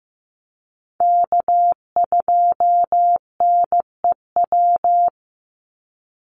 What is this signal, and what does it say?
Morse code, heard as a keyed tone, about 15 wpm, 700 Hz. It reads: K2NEW
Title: USA Call Signs 15wpm